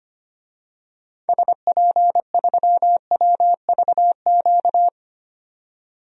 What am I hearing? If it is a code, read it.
Morse code, heard as a keyed tone, about 25 wpm, 700 Hz. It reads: SP3WVQ
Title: DXCC-100 Call Signs - 6 Characters - 25wpm